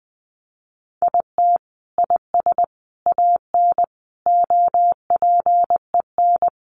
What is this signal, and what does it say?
Morse code, heard as a keyed tone, about 20 wpm, 700 Hz. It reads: IT IS AN OPEN
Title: Sentences from Top 1300 Words 20wpm